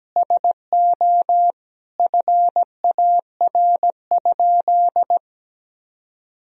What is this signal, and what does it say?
Morse code, heard as a keyed tone, about 17 wpm, 700 Hz. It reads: SO FAR?
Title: Sentences from Top 300 Words and Abbreviations 17wpm